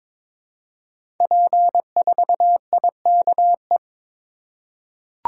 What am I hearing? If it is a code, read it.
Morse code, heard as a keyed tone, about 22 wpm, 700 Hz. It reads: P4IKE
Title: Code Groups - 5 Letter:Numbers - 22wpm